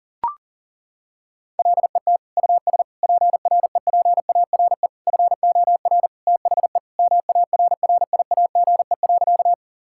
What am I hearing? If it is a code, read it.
Morse code, heard as a keyed tone, about 40 wpm, 700 Hz. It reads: LET US PREPARE FOR THE MARRIAGE.
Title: Sentences from Top 1000 Words - Review Part 6 - 40wpm